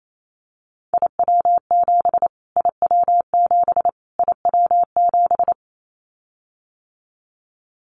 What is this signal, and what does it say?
Morse code, heard as a keyed tone, about 28 wpm, 700 Hz. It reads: IW7 IW7 IW7
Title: Callsign Prefixes for Top 25 DXCC - Repeated 3x - 28wpm